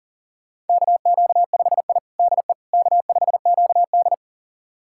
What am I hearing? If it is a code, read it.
Morse code, heard as a keyed tone, about 40 wpm, 700 Hz. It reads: KQ5I DE K5QD
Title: USA Callsign DE Callsign - 4 Characters - 40wpm